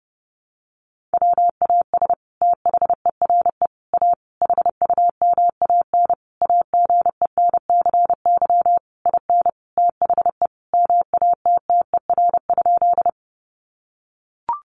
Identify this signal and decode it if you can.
Morse code, heard as a keyed tone, about 30 wpm, 700 Hz. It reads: WAS THERE A HUMAN AGENCY IN THE MATTER?
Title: Sentences from Top 1300 Words 30wpm